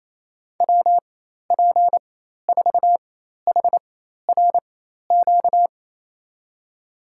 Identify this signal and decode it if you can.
Morse code, heard as a keyed tone, about 28 wpm, spaced as for 13 wpm, 700 Hz. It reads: WP4HRQ
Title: USA Call Signs - 6 Characters - 4x Character Spacing - 28wpm